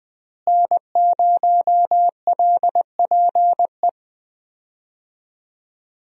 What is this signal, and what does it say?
Morse code, heard as a keyed tone, about 20 wpm, 700 Hz. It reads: N0LPE
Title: USA Call Signs - 5 Characters - Repeated 2x - 20wpm